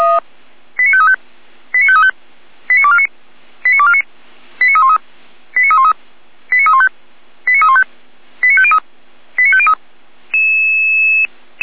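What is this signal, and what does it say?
Five-tone selective call (ZVEI), heard as quick runs of five tones, tones of about 70 ms. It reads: 89535 89535 89249 89249 89313 89313 89315 89315 89503 89503
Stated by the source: ZVEI five-tone selective call on 455, received by SDR.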